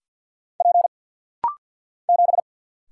Morse code, heard as a keyed tone, about 50 wpm, 700 Hz. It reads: P 7
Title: Single Letter-Number - Rapid-Fire 50wpm